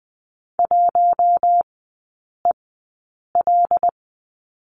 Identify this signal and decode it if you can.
Morse code, heard as a keyed tone, about 20 wpm, spaced as for 8 wpm, 700 Hz. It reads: 1EL
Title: 133 Lesson 22e - Question Mark - Three Characters 20wpm